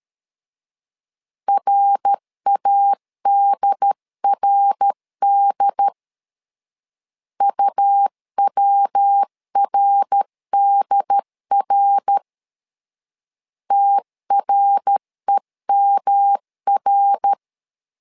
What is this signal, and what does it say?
Morse code, heard as a keyed tone, about 13 wpm, 800 Hz. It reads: RADRD UWRDR TREMR
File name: LEZIONI / 3 MWRO / 3 MWR 12wpm 1711